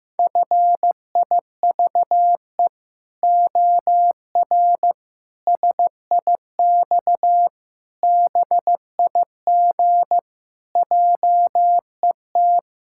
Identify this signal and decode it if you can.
Morse code, heard as a keyed tone, about 16 wpm, 700 Hz. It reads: FIVE OR SIX BIG JET
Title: Pangram Challenge v2 - 15wpm